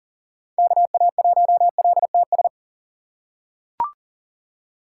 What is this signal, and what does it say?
Morse code, heard as a keyed tone, about 40 wpm, 700 Hz. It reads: KA1LTS
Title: USA Call Signs - 6 Characters - 2x Character Spacing - 40wpm